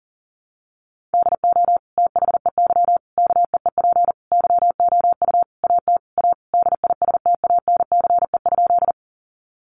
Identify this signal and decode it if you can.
Morse code, heard as a keyed tone, about 37 wpm, 700 Hz. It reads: DO THEY KEEP YOU AT A DISTANCE?
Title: Sentences from Top 1200 Words 40wpm